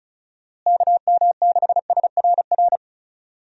Morse code, heard as a keyed tone, about 36 wpm, 700 Hz. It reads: KM6SRR
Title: USA Call Signs - 6 Characters - Repeated 3x - 35wpm